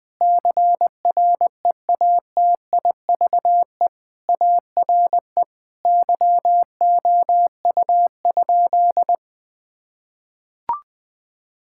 Morse code, heard as a keyed tone, about 20 wpm, 700 Hz. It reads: CREATIVE ARE YOU?
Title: Sentences from Top 1000 Words - Review Part 5 - 20wpm